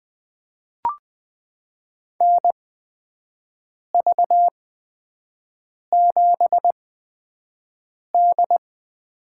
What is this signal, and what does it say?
Morse code, heard as a keyed tone, about 20 wpm, spaced as for 5 wpm, 700 Hz. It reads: NV7D
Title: USA Call Signs - 8x Character Spacing - 20wpm